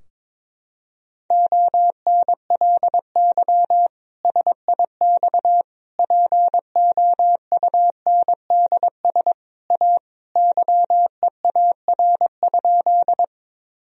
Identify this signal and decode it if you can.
Morse code, heard as a keyed tone, about 22 wpm, 700 Hz. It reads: ONLY SIX POUNDS A YEAR?
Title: Sentences from Top 1000 Words - Review Part 2 - 22wpm